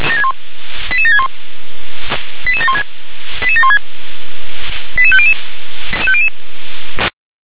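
Five-tone selective call (ZVEI), heard as quick runs of five tones, tones of about 80 ms. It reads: 88761 88716 88716 88500
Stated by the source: ZVEI five-tone selective call for code 88795, received by SDR.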